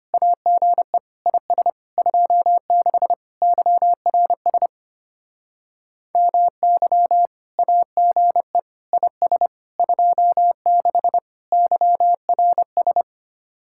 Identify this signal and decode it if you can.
Morse code, heard as a keyed tone, about 32 wpm, 690 Hz. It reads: AGE IS 26 YRS MY AGE IS 26 YRS